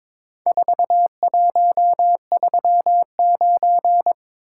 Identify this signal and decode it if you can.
Morse code, heard as a keyed tone, about 23 wpm, 700 Hz. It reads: 4139
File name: practice / 22wpm / Four Letter-Number Combinations - 22wpm